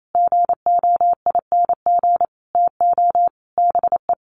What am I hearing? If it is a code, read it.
Morse code, heard as a keyed tone, about 28 wpm, 700 Hz. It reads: GOING TO BE
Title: Sets of 5 Words Encore2 28wpm